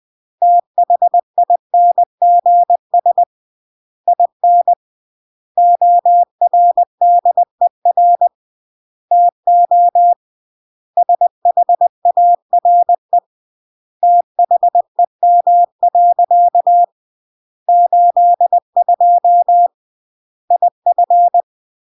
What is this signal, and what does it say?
Morse code, heard as a keyed tone, about 20 wpm, 700 Hz. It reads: THINGS IN ORDER TO SHARE THEM. 82 IF
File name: CW / jodorowsky / 82 commandments 20-20